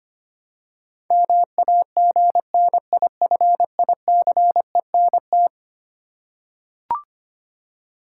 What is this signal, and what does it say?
Morse code, heard as a keyed tone, about 25 wpm, 700 Hz. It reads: MAGNIFICENT
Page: morse-code-ninja-practice-25wpm